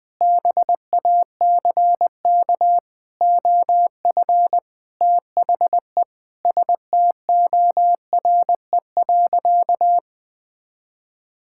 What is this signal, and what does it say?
Morse code, heard as a keyed tone, about 20 wpm, 700 Hz. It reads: BACK OF THE STORE.
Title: Sentences from Top 1000 Words - Review Part 5 - 20wpm